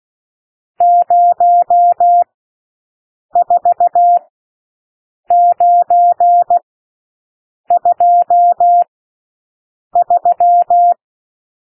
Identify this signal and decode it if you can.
Morse code, heard as a keyed tone, about 16 wpm, spaced as for 6 wpm, 700 Hz. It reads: 04923
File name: SOUND NUMBERS 16 wpm